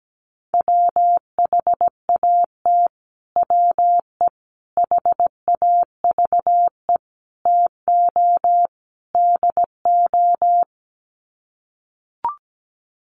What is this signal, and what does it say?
Morse code, heard as a keyed tone, about 17 wpm, 700 Hz. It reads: WHAT WE HAVE TO DO
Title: Sets of 5 Words Encore 17wpm